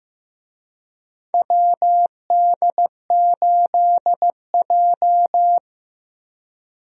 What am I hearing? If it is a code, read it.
Morse code, heard as a keyed tone, about 15 wpm, 700 Hz. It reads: WD8J
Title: USA Call Signs - 4 Characters - Repeated 3x - 15wpm